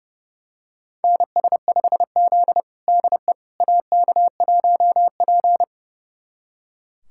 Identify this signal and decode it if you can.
Morse code, heard as a keyed tone, about 30 wpm, 700 Hz. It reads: NS5Z DE AK1P
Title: USA Callsign DE Callsign - 4 Characters - 30wpm